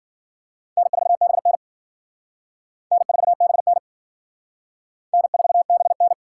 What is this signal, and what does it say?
Morse code, heard as a keyed tone, about 50 wpm, 700 Hz. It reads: N4BN N4BN N4BN
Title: Speed Racing - USA 1x2 Call Signs 40wpm